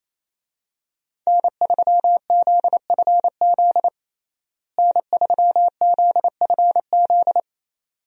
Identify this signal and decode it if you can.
Morse code, heard as a keyed tone, about 28 wpm, 700 Hz. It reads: N3ZFZ N3ZFZ
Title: USA Call Signs - 5 Characters - Repeated 2x - 28wpm